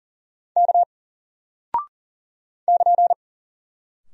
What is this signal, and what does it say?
Morse code, heard as a keyed tone, about 40 wpm, 700 Hz. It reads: K <KN>
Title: Single Character (Letter:Number:Punctuation:Prosign) Alt 40wpm